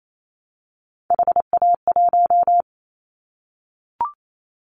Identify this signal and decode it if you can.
Morse code, heard as a keyed tone, about 28 wpm, 700 Hz. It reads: HA1
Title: Callsign Prefixes for Top 50 DXCC - 28wpm